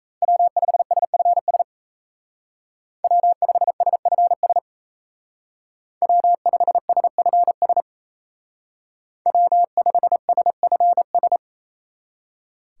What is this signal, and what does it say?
Morse code, heard as a keyed tone, about 33 wpm, 700 Hz. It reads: W5SFS W5SFS W5SFS W5SFS
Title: Speed Racing - USA 1x3 Call Signs 28wpm